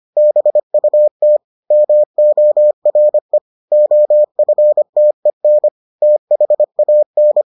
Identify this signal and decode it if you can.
Morse code, heard as a keyed tone, about 25 wpm, 600 Hz. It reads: BUT MORE OFTEN THAN